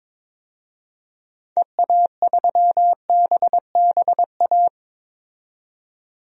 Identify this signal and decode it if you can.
Morse code, heard as a keyed tone, about 22 wpm, 700 Hz. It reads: EA3BBA
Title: DXCC-100 Call Signs - 6 Characters - 22wpm